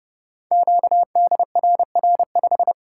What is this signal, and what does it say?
Morse code, heard as a keyed tone, about 29 wpm, 700 Hz. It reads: QDRR5
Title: Code Groups - 5 Letter:Numbers - 30wpm